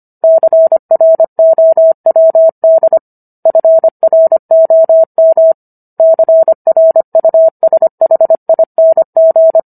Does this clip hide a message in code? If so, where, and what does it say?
Yes: Morse code from 0.2 to 9.6 s, heard as a keyed tone, about 25 wpm, 650 Hz. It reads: CROWD FROM CRUSHING